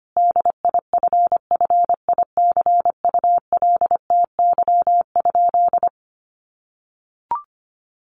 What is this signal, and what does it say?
Morse code, heard as a keyed tone, about 25 wpm, 700 Hz. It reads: DIFFICULTY?
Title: Sentences from Top 1500 Words 25wpm